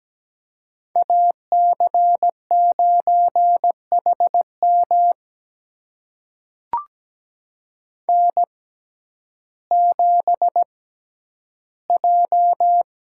Morse code, heard as a keyed tone, about 17 wpm, spaced as for 10 wpm, 700 Hz. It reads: AC9HM N7J
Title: USA Call Signs - 5 Characters - 6x Character Spacing - 17wpm